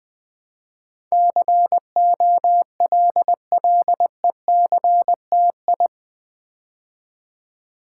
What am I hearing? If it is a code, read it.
Morse code, heard as a keyed tone, about 20 wpm, 700 Hz. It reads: COLLECTI
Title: Build A Word - 4 Letters At A Time - 10-Letter Words - 20wpm